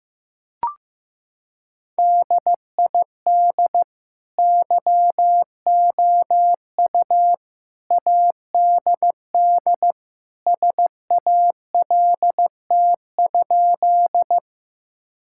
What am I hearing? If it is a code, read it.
Morse code, heard as a keyed tone, about 16 wpm, 700 Hz. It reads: DID YOU ADD SALT?
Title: Sentences from Top 1000 Words - Review Part 5 - 15wpm